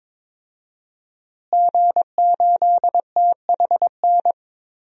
Morse code, heard as a keyed tone, about 22 wpm, 700 Hz. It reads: G8THN
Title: DXCC-100 Call Signs - 5 Characters - Repeated 2x - 22wpm